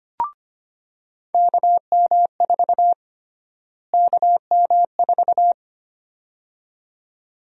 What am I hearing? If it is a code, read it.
Morse code, heard as a keyed tone, about 25 wpm, 700 Hz. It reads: KM4 KM4
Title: USA Call Sign Prefixes - Repeated 2x - 25wpm